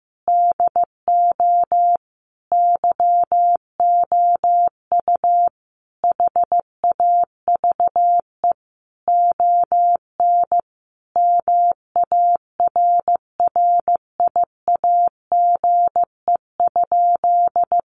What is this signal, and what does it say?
Morse code, heard as a keyed tone, about 15 wpm, 700 Hz. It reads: DO YOU HAVE ON MARRIAGE?
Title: Sentences from Top 1200 Words 15wpm